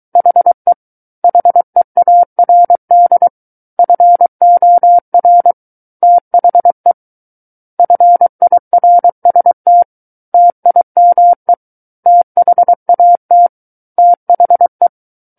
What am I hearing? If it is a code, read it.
Morse code, heard as a keyed tone, about 23 wpm, 700 Hz. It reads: HE HEARD FOR THE FIRST TIME THAT THE
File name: Book One / 14-Chapter 14 - 23 WPM